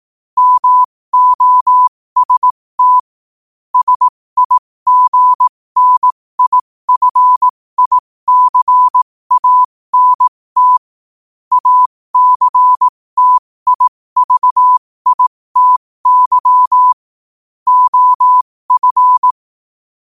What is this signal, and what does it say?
Morse code, heard as a keyed tone, about 18 wpm, spaced as for 15 wpm, 1000 Hz. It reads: MOST SIGNIFICANT ACTIVITY OF
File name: qotdfiles / 15wpm